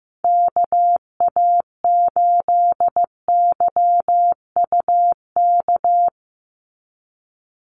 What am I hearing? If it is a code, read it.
Morse code, heard as a keyed tone, about 15 wpm, 700 Hz. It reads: KA8YUK
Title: USA Call Signs 15wpm